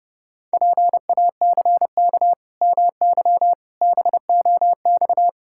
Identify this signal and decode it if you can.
Morse code, heard as a keyed tone, about 30 wpm, 700 Hz. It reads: PACK MY BOX
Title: Pangram Challenge v2 - 30wpm